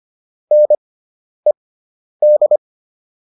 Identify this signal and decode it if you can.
Morse code, heard as a keyed tone, about 25 wpm, spaced as for 10 wpm, 600 Hz. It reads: NED